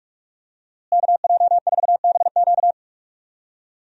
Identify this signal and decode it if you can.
Morse code, heard as a keyed tone, about 44 wpm, 700 Hz. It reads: KJ4BQ
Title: USA Call Signs - 5 Characters - Repeated 2x - 45wpm